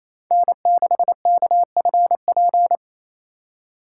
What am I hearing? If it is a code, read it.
Morse code, heard as a keyed tone, about 33 wpm, 700 Hz. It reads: N6KFP
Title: Speed Racing - USA Call Signs 28wpm